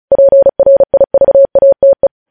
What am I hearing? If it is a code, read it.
Morse code, heard as a keyed tone, about 36 wpm, 550 Hz. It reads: PRIVATE
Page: News headlines in Morse code at 35 wpm